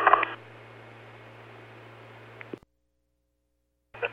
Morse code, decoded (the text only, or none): none